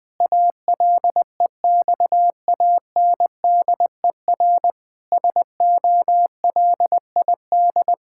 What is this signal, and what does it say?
Morse code, heard as a keyed tone, about 19 wpm, 700 Hz. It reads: ALEXANDER SOLID
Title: K1USN SST QSO Practice 20wpm